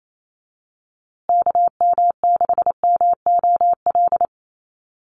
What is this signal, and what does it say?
Morse code, heard as a keyed tone, about 28 wpm, 700 Hz. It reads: KM6MOL
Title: USA Call Signs Repeated 2x 28wpm